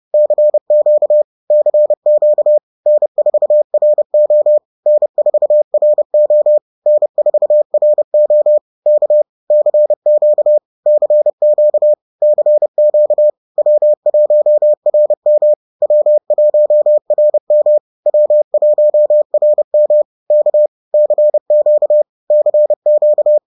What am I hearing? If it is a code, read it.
Morse code, heard as a keyed tone, about 30 wpm, 600 Hz. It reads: CQ CQ N4RO N4RO N4RO K CQ CQ CQ W1RM W1RM W1RM K CQ CQ